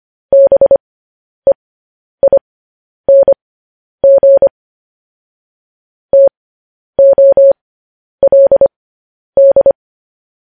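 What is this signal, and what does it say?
Morse code, heard as a keyed tone, about 25 wpm, spaced as for 10 wpm, 550 Hz. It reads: BEING TOLD
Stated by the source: Morse code news headlines sent at 10 words per minute.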